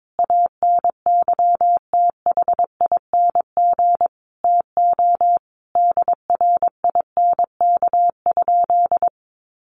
Morse code, heard as a keyed tone, about 22 wpm, 700 Hz. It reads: ANYTHING TO DRINK?
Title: Sentences from Top 1500 Words 22wpm